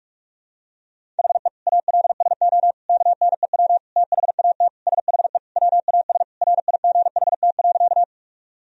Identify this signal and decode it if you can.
Morse code, heard as a keyed tone, about 46 wpm, 700 Hz. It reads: HE ALSO KNEW THAT SHE WAS RIGHT.
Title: Sentences from Top 1000 Words - Review Part 1 - 45wpm